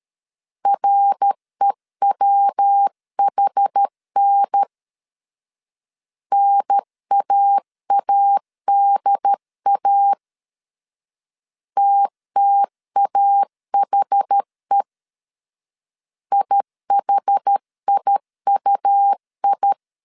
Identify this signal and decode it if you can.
Morse code, heard as a keyed tone, about 13 wpm, 800 Hz. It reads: REWHN NAADA TTAHE IHIUI
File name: CW / DET / 1 3 1500C S A 12wpm 2905